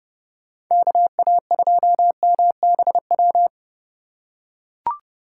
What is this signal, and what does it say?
Morse code, heard as a keyed tone, about 29 wpm, 700 Hz. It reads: KA2MBW